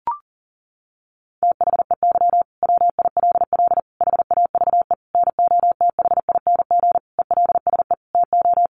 Morse code, heard as a keyed tone, about 40 wpm, 700 Hz. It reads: THEY WILL HAVE NOTHING ELSE TO
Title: Sentences from Top 800 Words 40wpm